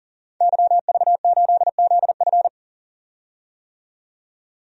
Morse code, heard as a keyed tone, about 40 wpm, 700 Hz. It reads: YV8ZF
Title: DXCC-100 Call Signs - 5 Characters - 40wpm